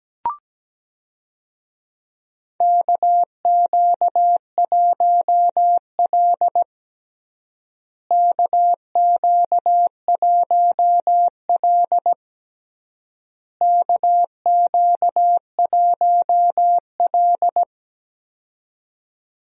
Morse code, heard as a keyed tone, about 17 wpm, 700 Hz. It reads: KQ1L KQ1L KQ1L
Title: USA Call Signs - 4 Characters - Repeated 3x - 17wpm